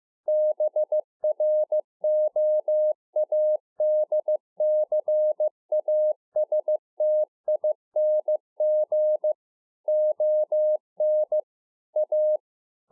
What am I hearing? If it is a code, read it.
Morse code, heard as a keyed tone, about 15 wpm, 610 Hz. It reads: BROADCASTING ON A